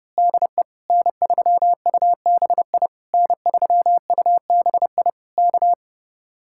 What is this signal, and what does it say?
Morse code, heard as a keyed tone, about 28 wpm, 700 Hz. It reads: DE N3UBI N3UBI K
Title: Speed-Racing Common QSO Phrases Encore - 30wpm